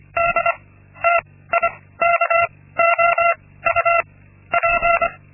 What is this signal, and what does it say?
Morse code, heard as a keyed tone, about 25 wpm, 680 Hz. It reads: D T I K O U P
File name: Linksmakalnis-2